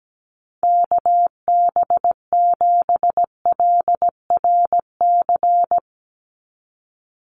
Morse code, heard as a keyed tone, about 17 wpm, 700 Hz. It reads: KB7LRC
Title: USA Call Signs 17wpm